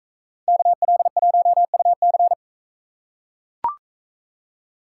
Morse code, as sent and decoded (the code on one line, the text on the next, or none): -.- .-.. .---- ..- -.-.
KL1UC